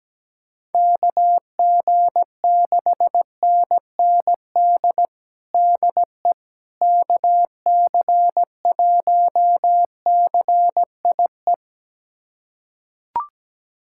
Morse code, heard as a keyed tone, about 17 wpm, 700 Hz. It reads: KG6NND DE KC1CIE
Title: USA Callsign DE Callsign - 6 Characters - 17wpm